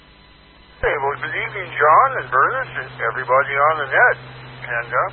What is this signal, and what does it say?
Emergency Alert System (SAME) tones, heard as harsh buzzing bursts, with speech. It is heard from 1.2 to 4.2 s.